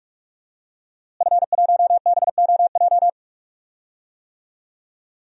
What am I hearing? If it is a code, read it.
Morse code, heard as a keyed tone, about 45 wpm, 700 Hz. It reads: F1BOJ
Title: DXCC-100 Call Signs - 5 Characters - Repeated 3x - 45wpm